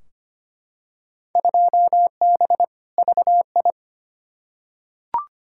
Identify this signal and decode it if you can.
Morse code, heard as a keyed tone, about 25 wpm, 700 Hz. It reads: 2B VI
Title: Field Day Exchanges Encore - 25wpm